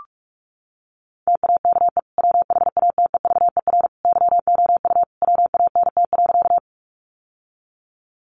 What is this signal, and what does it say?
Morse code, heard as a keyed tone, about 44 wpm, 700 Hz. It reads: TAKE WHATEVER YOU WANT.
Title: Sentences from Top 1100 Words 45wpm